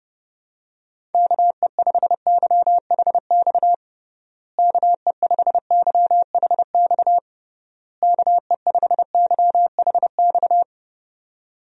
Morse code, heard as a keyed tone, about 32 wpm, 700 Hz. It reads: KE5YHX KE5YHX KE5YHX
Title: USA Call Signs - 6 Characters - Repeated 3x - 30wpm